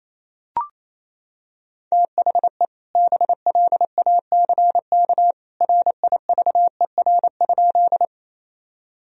Morse code, heard as a keyed tone, about 28 wpm, 700 Hz. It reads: THE BLACK RIVER?
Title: Sentences from Top 1000 Words - Review Part 2 - 28wpm